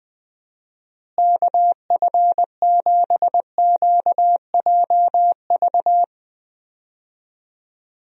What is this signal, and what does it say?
Morse code, heard as a keyed tone, about 20 wpm, 700 Hz. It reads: KF7QJV
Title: USA Call Signs - 6 Characters - Repeated 3x - 20wpm